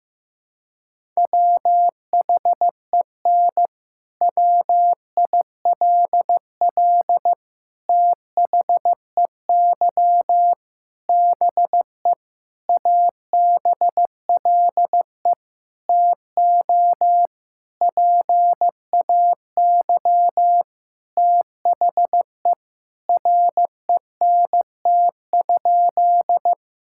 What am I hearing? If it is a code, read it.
Morse code, heard as a keyed tone, about 16 wpm, 700 Hz. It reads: WHEN WILL THEY BE ABLE TO PAY THE RENT?
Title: Sentences from Top 1000 Words - Review Part 5 - 15wpm